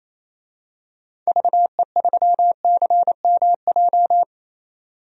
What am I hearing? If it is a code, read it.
Morse code, heard as a keyed tone, about 28 wpm, 700 Hz. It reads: VE3CMJ
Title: DXCC-100 Call Signs - 6 Characters - 28wpm